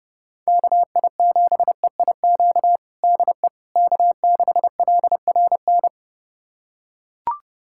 Morse code, heard as a keyed tone, about 30 wpm, 700 Hz. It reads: KI7EIQ DE K6LRN
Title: USA Callsign DE Callsign 30wpm